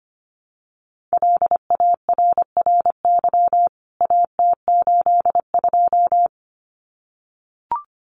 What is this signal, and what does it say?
Morse code, heard as a keyed tone, about 25 wpm, 700 Hz. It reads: LARRY AT82
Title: CWT Exchange 25wpm